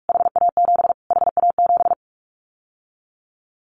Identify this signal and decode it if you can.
Morse code, heard as a keyed tone, about 44 wpm, 700 Hz. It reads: HA7 HA7
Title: Callsign Prefixes for Top 25 DXCC - Repeated 3x - 45wpm